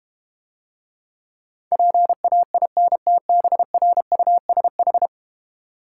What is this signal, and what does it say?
Morse code, heard as a keyed tone, about 32 wpm, 700 Hz. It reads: PAINTBRUSH